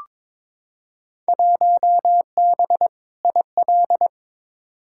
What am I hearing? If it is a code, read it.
Morse code, heard as a keyed tone, about 22 wpm, 700 Hz. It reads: 1B IL
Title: Field Day Exchanges - 22wpm